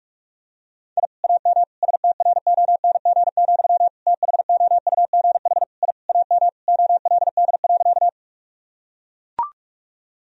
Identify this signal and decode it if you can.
Morse code, heard as a keyed tone, about 41 wpm, 700 Hz. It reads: I AM STRONG, THOUGH I AM OLD.
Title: Sentences from Top 1000 Words - Review Part 3 - 45wpm